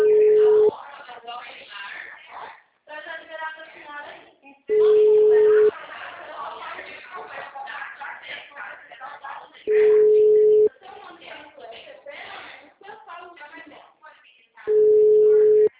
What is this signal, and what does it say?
Ringback tone (Europe, 425 Hz), with speech.
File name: out- -2007- - - .83960